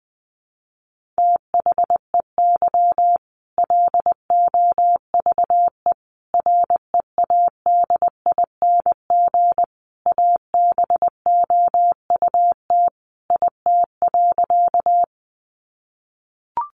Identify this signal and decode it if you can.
Morse code, heard as a keyed tone, about 20 wpm, 700 Hz. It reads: THEY LOVE READING ABOUT IT.
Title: Sentences from Top 600 Words 20wpm